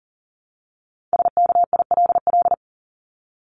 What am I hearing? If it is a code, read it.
Morse code, heard as a keyed tone, about 40 wpm, 700 Hz. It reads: SKILL
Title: Top 1001-1250 Words 40wpm